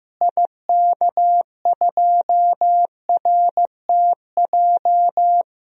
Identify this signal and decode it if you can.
Morse code, heard as a keyed tone, about 15 wpm, 700 Hz. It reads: IK2RTJ
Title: DXCC-100 Call Signs - 6 Characters - Repeated 2x - 15wpm